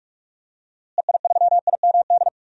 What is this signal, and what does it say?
Morse code, heard as a keyed tone, about 41 wpm, 700 Hz. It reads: EI3IMD
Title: DXCC-100 Call Signs - 6 Characters - Repeated 2x - 45wpm